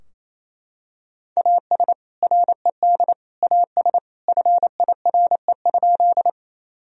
Morse code, heard as a keyed tone, about 28 wpm, 700 Hz. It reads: AS RED AS FIRE?
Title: Sentences from Top 1000 Words - Review Part 2 - 28wpm